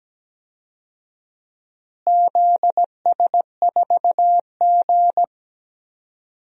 Morse code, heard as a keyed tone, about 17 wpm, 700 Hz. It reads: ZS4G
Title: DXCC-100 Call Signs - 4 Characters - Repeated 2x - 17wpm